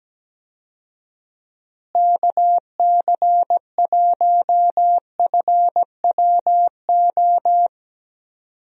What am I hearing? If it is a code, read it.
Morse code, heard as a keyed tone, about 17 wpm, 700 Hz. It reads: KC1FWO
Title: USA Call Signs - 6 Characters - Repeated 3x - 17wpm